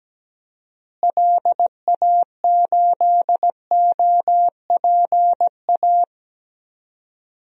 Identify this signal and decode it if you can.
Morse code, heard as a keyed tone, about 17 wpm, 700 Hz. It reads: LA8OPA
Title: DXCC-100 Call Signs - 6 Characters - Repeated 2x - 17wpm